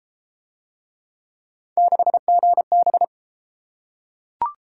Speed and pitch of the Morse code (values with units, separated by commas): 28 wpm, 700 Hz